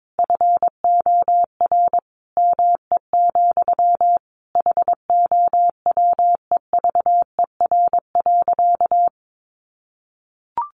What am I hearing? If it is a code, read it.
Morse code, heard as a keyed tone, about 22 wpm, 700 Hz. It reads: FOR ME, HOWEVER.
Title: Sentences from Top 600 Words 22wpm